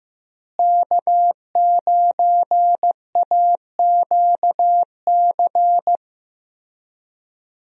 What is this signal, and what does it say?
Morse code, heard as a keyed tone, about 15 wpm, 700 Hz. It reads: K9AQC
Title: USA Call Signs - 5 Characters - Repeated 2x - 15wpm